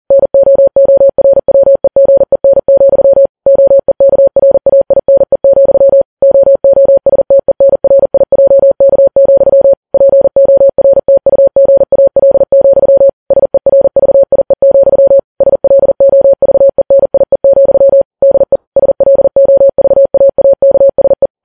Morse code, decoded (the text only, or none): NOORWEGEN, OEKRAINE, OOSTENRIJK, PORTUGAL, SERVIE, SLOVENIE, DE SLOVAAKSE